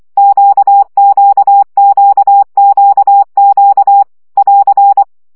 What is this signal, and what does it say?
Morse code, heard as a keyed tone, about 24 wpm, 800 Hz. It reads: QQQQQ <AR>
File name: CW gyakorlas q 20 baud